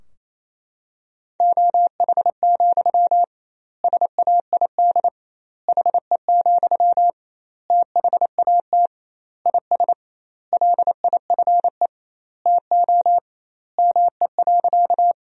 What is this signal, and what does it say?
Morse code, heard as a keyed tone, about 27 wpm, 700 Hz. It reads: OH, SAID HE, THAT IS LIFE TO ME.
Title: 2x Word-Spacing - Sentences from Top 500 Words - 28wpm